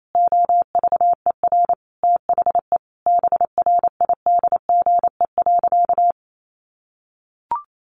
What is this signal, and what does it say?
Morse code, heard as a keyed tone, about 28 wpm, 700 Hz. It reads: OVER THE BRIDGE.
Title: Sentences from Top 1500 Words 28wpm